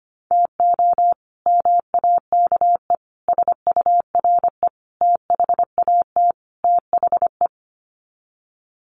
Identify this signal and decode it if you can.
Morse code, heard as a keyed tone, about 25 wpm, 700 Hz. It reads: TO MAKE SURE THAT THE
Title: Sets of 5 Words 25wpm